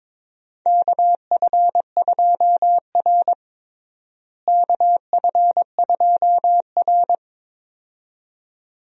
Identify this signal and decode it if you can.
Morse code, heard as a keyed tone, about 22 wpm, 700 Hz. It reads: KF2R KF2R
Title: USA Call Signs - 4 Characters - Repeated 2x - 22wpm